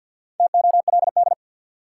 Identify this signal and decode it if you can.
Morse code, heard as a keyed tone, about 46 wpm, 700 Hz. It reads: TOLD
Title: Sentences from Top 400 Words - New Word Drill - 50wpm